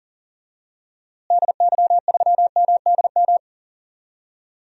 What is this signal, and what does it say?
Morse code, heard as a keyed tone, about 37 wpm, 700 Hz. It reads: DY3MDM
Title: DXCC-100 Call Signs - 6 Characters - Repeated 2x - 40wpm